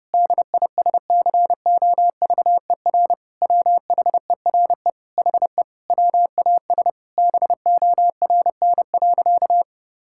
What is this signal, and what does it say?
Morse code, heard as a keyed tone, about 30 wpm, 700 Hz. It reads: DISCOVER WHERE HE WAS BORN.
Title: Sentences from Top 1600 Words 30wpm